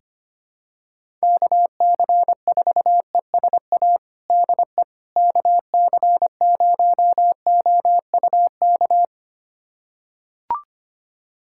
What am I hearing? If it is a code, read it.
Morse code, heard as a keyed tone, about 25 wpm, 700 Hz. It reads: KC4ESA DE KC0OUK
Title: USA Callsign DE Callsign - 6 Characters - 25wpm